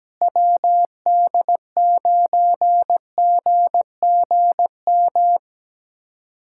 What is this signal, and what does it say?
Morse code, heard as a keyed tone, about 17 wpm, 700 Hz. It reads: WD9GGM
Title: USA Callsign DE Callsign - 6 Characters - 17wpm